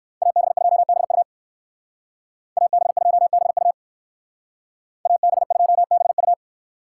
Morse code, heard as a keyed tone, about 57 wpm, 700 Hz. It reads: AB2BU AB2BU AB2BU